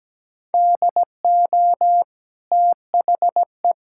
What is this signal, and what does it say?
Morse code, heard as a keyed tone, about 17 wpm, 700 Hz. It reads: DO THE
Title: Sentences from Top 1000 Words - Review Part 5 - 17wpm